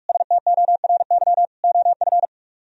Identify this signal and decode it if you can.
Morse code, heard as a keyed tone, about 45 wpm, 700 Hz. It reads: STORY OF